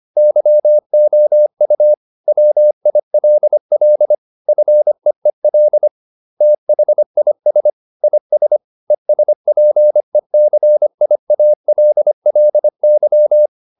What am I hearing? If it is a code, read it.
Morse code, heard as a keyed tone, about 25 wpm, 600 Hz. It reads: YOU WILL FEEL THIS IS ESPECIALLY